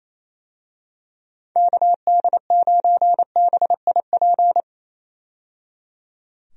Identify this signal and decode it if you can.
Morse code, heard as a keyed tone, about 28 wpm, 700 Hz. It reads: KD9BIP
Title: USA Call Signs - 6 Characters - 28wpm